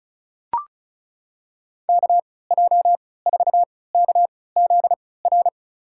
Morse code, heard as a keyed tone, about 36 wpm, spaced as for 20 wpm, 700 Hz. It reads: KJ4KZR
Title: USA Call Signs - 6 Characters - 3x Character Spacing - 35wpm